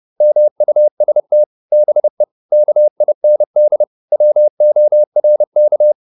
Morse code, heard as a keyed tone, about 30 wpm, 600 Hz. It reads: MUST BE KIND WORK